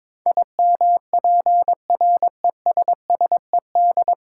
Morse code, heard as a keyed tone, about 22 wpm, 700 Hz. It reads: IMPRESSED
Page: Morse Code Ninja practice 22wpm